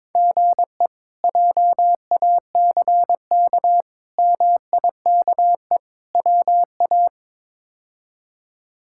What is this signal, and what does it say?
Morse code, heard as a keyed tone, about 22 wpm, 700 Hz. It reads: GE JACK MIKE WA
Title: K1USN SST QSO Practice - 22wpm